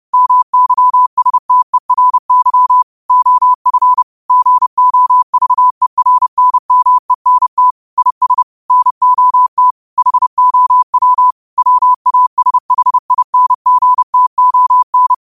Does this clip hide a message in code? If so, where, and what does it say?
Yes: Morse code from 0.1 to 15.1 s, heard as a keyed tone, about 30 wpm, 1000 Hz. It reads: MYSTERY OF GOVERNMENT IS NOT HOW WASHINGTON